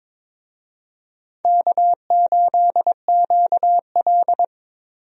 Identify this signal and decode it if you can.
Morse code, heard as a keyed tone, about 21 wpm, 700 Hz. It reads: K8QL
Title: USA Call Signs - 4 Characters - Repeated 3x - 22wpm